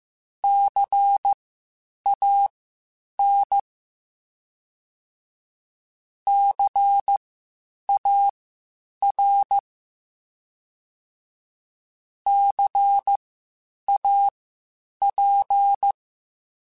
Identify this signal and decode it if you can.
Morse code, heard as a keyed tone, about 15 wpm, spaced as for 8 wpm, 800 Hz. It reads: CAN CAR CAP
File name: Spaced Alphas / Spaced Alphas mp3s / Cs sp @13